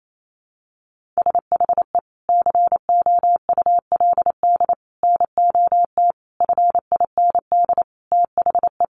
Morse code, heard as a keyed tone, about 28 wpm, 700 Hz. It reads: SHE COULD NOT FIND THE
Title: Sentences from Top 1100 Words 28wpm